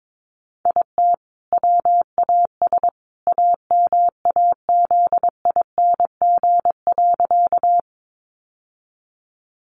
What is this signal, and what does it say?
Morse code, heard as a keyed tone, about 22 wpm, 700 Hz. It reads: IT WAS AMAZING.
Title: Sentences from Top 1500 Words 22wpm